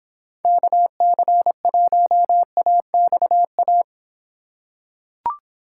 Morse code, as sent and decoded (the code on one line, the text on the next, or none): -.- -.-. .---- .- -..- .-
KC1AXA